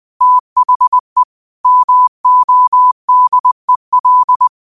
Morse code, heard as a keyed tone, about 20 wpm, 1000 Hz. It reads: THE MODEL
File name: qotdfiles / 20wpm